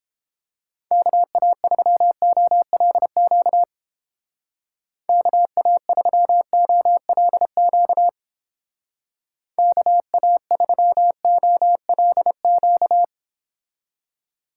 Morse code, heard as a keyed tone, about 30 wpm, 700 Hz. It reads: KA3OLQ KA3OLQ KA3OLQ